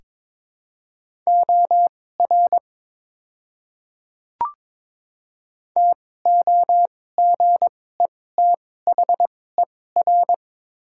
Morse code, heard as a keyed tone, about 22 wpm, spaced as for 16 wpm, 700 Hz. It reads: OR TOGETHER
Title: 2x Character Spacing - Top 300 Words - 22wpm